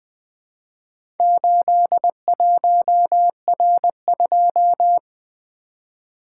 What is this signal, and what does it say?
Morse code, heard as a keyed tone, about 20 wpm, 700 Hz. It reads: 81R2
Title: Four Letter-Number Combinations - 20wpm